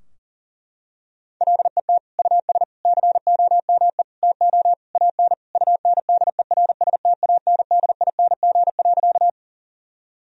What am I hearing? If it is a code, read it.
Morse code, heard as a keyed tone, about 37 wpm, 700 Hz. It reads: LET US COME TO AN UNDERSTANDING.
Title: Sentences from Top 1000 Words - Review Part 3 - 40wpm